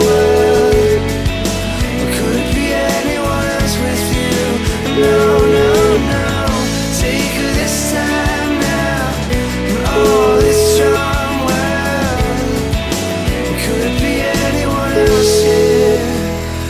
Ringback tone (Europe, 425 Hz), with noise.